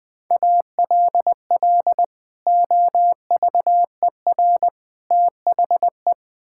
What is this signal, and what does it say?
Morse code, heard as a keyed tone, about 20 wpm, 700 Hz. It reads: ALL OVER THE
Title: Build A Phrase - 3-Word Phrases - 20wpm